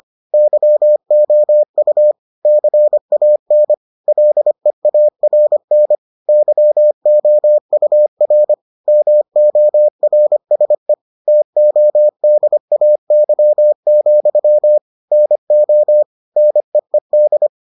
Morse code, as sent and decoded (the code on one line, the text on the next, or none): -.-- --- ..- / -.-. .- -. / .-.. .- .-. -. / -.-- --- ..- .-. / -- --- .-. ... / - --- -.. .- -.-- --..-- / -. --- / -. . -..
YOU CAN LEARN YOUR MORSE TODAY, NO NEED